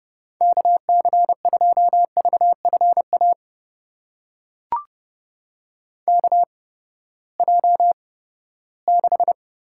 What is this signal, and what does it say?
Morse code, heard as a keyed tone, about 30 wpm, spaced as for 17 wpm, 700 Hz. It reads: KC2VFA KJ6
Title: USA Call Signs - 6 Characters - 8x Character Spacing - 30wpm